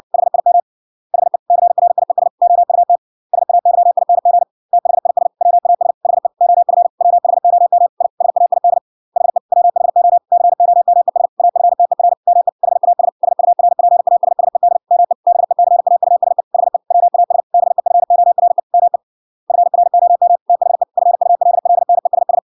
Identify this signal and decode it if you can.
Morse code, heard as a keyed tone, about 61 wpm, 700 Hz. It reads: HIM HE CRIES OUT SAYING THIS WAS HE OF WHOM I SAID HE WHO COMES AFTER ME HAS SURPASSED ME BECAUSE HE WAS BEFORE ME FROM THE FULLNESS